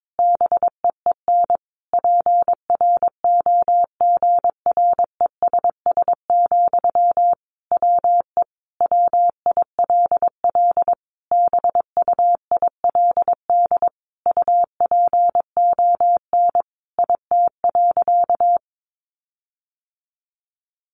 Morse code, heard as a keyed tone, about 22 wpm, 700 Hz. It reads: BEEN PROGRESS, WE WILL BUILD UPON IT.
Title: Sentences from Top 1400 Words 22wpm